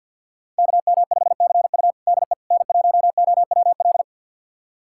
Morse code, heard as a keyed tone, about 51 wpm, 690 Hz. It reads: KM5CU DE N1OWL
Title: USA Callsign DE Callsign - 5 Characters - 50wpm